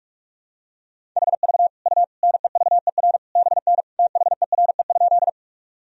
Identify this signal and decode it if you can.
Morse code, heard as a keyed tone, about 46 wpm, 700 Hz. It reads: HV U NEVER BN THERE?
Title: Sentences from Top 300 Words and Abbreviations 45wpm